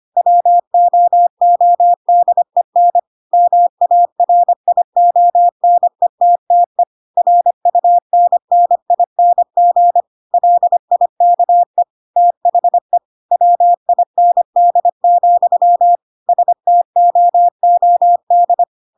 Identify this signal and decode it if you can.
Morse code, heard as a keyed tone, about 25 wpm, 700 Hz. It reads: WOODEN MARIONETTE RUNNING LIKE THE WIND, STOOD
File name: Pinocchio / Pinocchio ch3 25 25 ik1hge